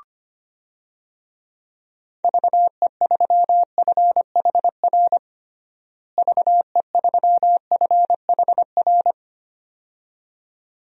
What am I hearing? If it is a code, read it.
Morse code, heard as a keyed tone, about 25 wpm, 700 Hz. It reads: VE3FHR VE3FHR
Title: DXCC-100 Call Signs - 6 Characters - Repeated 2x - 25wpm